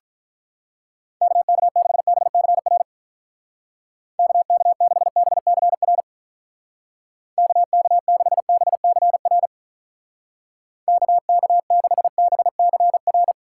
Morse code, heard as a keyed tone, about 45 wpm, 700 Hz. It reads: KK6BCR KK6BCR KK6BCR KK6BCR